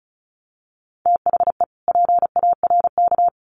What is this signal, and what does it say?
Morse code, heard as a keyed tone, about 34 wpm, 700 Hz. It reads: THE PARK
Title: Sentences from Top 1300 Words 35wpm